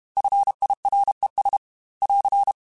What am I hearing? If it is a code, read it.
Morse code, heard as a keyed tone, about 32 wpm, 800 Hz. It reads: FIRES <AR>
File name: F Runs / F Runs mp3s / F Runs @30